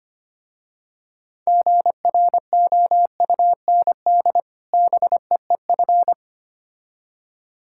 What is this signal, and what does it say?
Morse code, heard as a keyed tone, about 25 wpm, 700 Hz. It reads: GROUND BEEF
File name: practice / 25wpm / Difficult Words with Hints - 25wpm